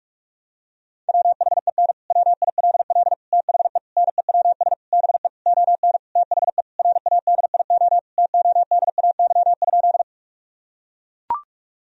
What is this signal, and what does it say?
Morse code, heard as a keyed tone, about 43 wpm, 700 Hz. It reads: WHEN WILL THE NEWS BE ON THE RADIO TODAY?
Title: Sentences from Top 1000 Words - Review Part 3 - 45wpm